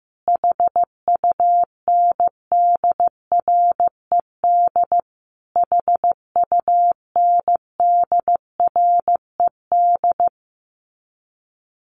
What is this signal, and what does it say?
Morse code, heard as a keyed tone, about 15 wpm, 700 Hz. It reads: HUNDRED HUNDRED
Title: Top 500 Words Repeated 15wpm